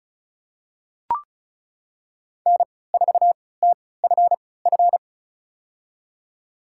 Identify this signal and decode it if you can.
Morse code, heard as a keyed tone, about 36 wpm, spaced as for 20 wpm, 700 Hz. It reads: N4TFF
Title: USA Call Signs - 3x Character Spacing - 35wpm